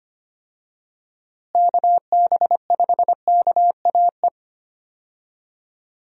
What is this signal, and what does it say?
Morse code, heard as a keyed tone, about 25 wpm, 700 Hz. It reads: KB5KAE
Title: USA Call Signs - 6 Characters - Repeated 2x - 25wpm